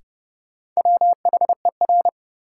Morse code, heard as a keyed tone, about 30 wpm, 700 Hz. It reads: WHER
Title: Sentences from Top 1500 Words - New Word Drill - 30wpm